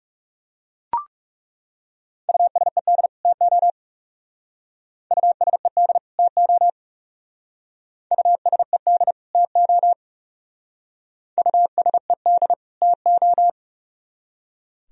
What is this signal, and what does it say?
Morse code, heard as a keyed tone, about 37 wpm, 700 Hz. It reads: USED TO USED TO USED TO USED TO
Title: Speed-Racing - Sets of 2 Words 30wpm